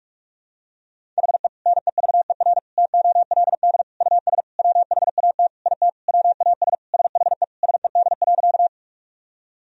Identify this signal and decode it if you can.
Morse code, heard as a keyed tone, about 42 wpm, 700 Hz. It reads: HE NEVER TOLD US WHAT IT WAS SHE SED.
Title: Sentences from Top 400 Words and Abbreviations 45wpm